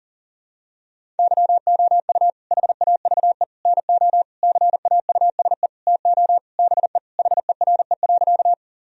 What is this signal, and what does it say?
Morse code, heard as a keyed tone, about 40 wpm, 700 Hz. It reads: YOU HAVE NO CAUSE TO BE HERE.
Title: Sentences from Top 1000 Words - Review Part 1 - 40wpm